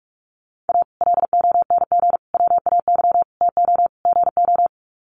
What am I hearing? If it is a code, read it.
Morse code, heard as a keyed tone, about 44 wpm, 700 Hz. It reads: A LONG WAY TO GO
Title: Sets of 5 Words 45wpm